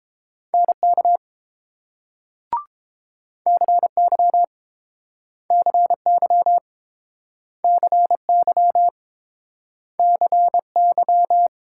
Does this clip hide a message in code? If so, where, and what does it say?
Yes: Morse code from 0.5 to 11.5 s, heard as a keyed tone, about 31 wpm, 700 Hz. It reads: NK CY CY CY CY